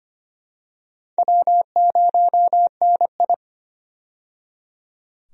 Morse code, heard as a keyed tone, about 25 wpm, 700 Hz. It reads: W0NI
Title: USA Call Signs - 4 Characters - 25wpm